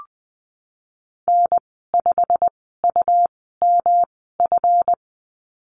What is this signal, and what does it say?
Morse code, heard as a keyed tone, about 20 wpm, spaced as for 15 wpm, 700 Hz. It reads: N5UMF
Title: 179 Lesson 29g - Slash - Call Signs 20wpm